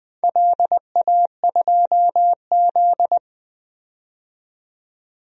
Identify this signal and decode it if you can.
Morse code, heard as a keyed tone, about 21 wpm, 700 Hz. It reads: LA2Z
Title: DXCC-100 Call Signs - 4 Characters - 20wpm